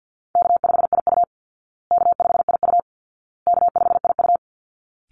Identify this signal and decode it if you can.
Morse code, heard as a keyed tone, about 50 wpm, 700 Hz. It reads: K5IU K5IU K5IU
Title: Call Signs Repeated 2x - 50wpm